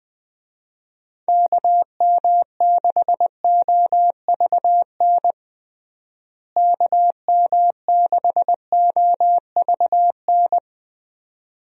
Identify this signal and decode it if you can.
Morse code, heard as a keyed tone, about 20 wpm, 700 Hz. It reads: KM6OVN KM6OVN
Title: USA Call Signs - 6 Characters - Repeated 2x - 20wpm